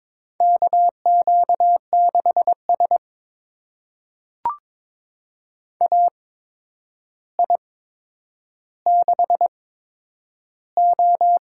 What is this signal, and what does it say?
Morse code, heard as a keyed tone, about 21 wpm, spaced as for 9 wpm, 700 Hz. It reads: KQ6S AI6O
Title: USA Call Signs - 4 Characters - 8x Character Spacing - 22wpm